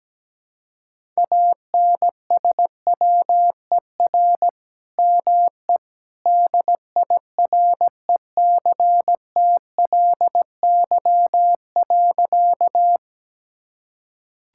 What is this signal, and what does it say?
Morse code, heard as a keyed tone, about 17 wpm, 700 Hz. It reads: ANSWER ME DIRECTLY.
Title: Sentences from Top 1000 Words - Review Part 5 - 17wpm